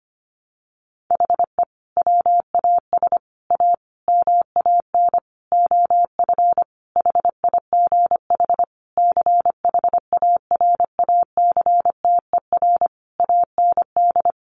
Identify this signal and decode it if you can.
Morse code, heard as a keyed tone, about 25 wpm, 700 Hz. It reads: HE WAS A MAN OF HIGH CHARACTER AND
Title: Sentences from Top 1200 Words 25wpm